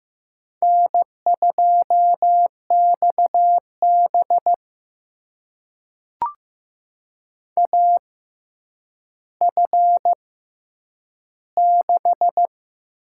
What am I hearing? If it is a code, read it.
Morse code, heard as a keyed tone, about 15 wpm, spaced as for 9 wpm, 700 Hz. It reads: N2XB AF6
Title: USA Call Signs - 6x Character Spacing - 15wpm